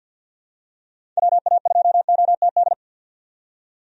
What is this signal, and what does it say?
Morse code, heard as a keyed tone, about 52 wpm, 700 Hz. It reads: WA2OTD
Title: USA Call Signs - 6 Characters - 50wpm